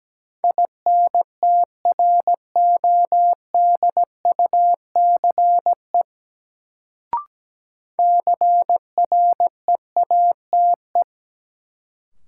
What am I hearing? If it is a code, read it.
Morse code, heard as a keyed tone, about 17 wpm, 700 Hz. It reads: INTRODUCE CREATE
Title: Sentences from Top 1600 Words - New Word Drill Alt - 17wpm